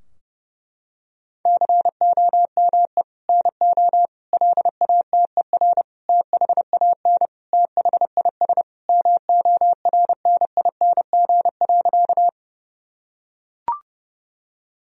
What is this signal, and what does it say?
Morse code, heard as a keyed tone, about 30 wpm, 700 Hz. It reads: COME NO LATER THAN THIS MORNING.
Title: Sentences from Top 1000 Words - Review Part 4 - 30wpm